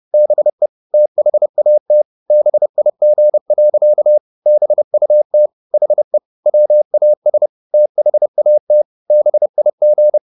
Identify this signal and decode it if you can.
Morse code, heard as a keyed tone, about 30 wpm, 600 Hz. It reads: BE THAT BIG. BUT HE WAS THAT BIG